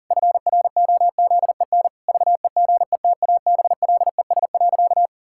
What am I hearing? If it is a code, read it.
Morse code, heard as a keyed tone, about 40 wpm, 700 Hz. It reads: FROZEN VEGETABLES.
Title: Pangram Challenge v2 - 40wpm